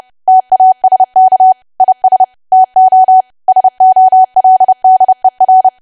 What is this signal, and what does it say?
Morse code, heard as a keyed tone, about 30 wpm, 750 Hz. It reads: TASK IS TO SOLDER
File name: cwa / W1AW-Level3 / 30WPM